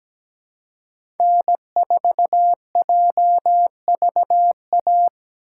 Morse code, heard as a keyed tone, about 17 wpm, 700 Hz. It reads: N4JVA
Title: USA Call Signs - 5 Characters - 4x Character Spacing - 17wpm